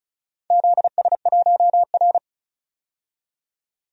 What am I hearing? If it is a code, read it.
Morse code, heard as a keyed tone, about 36 wpm, 700 Hz. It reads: ZS1R
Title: DXCC-100 Call Signs - 4 Characters - 35wpm